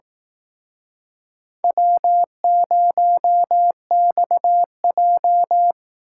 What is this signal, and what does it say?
Morse code, heard as a keyed tone, about 18 wpm, 700 Hz. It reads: W0XJ
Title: Speed Racing - USA 1x2 Call Signs 15wpm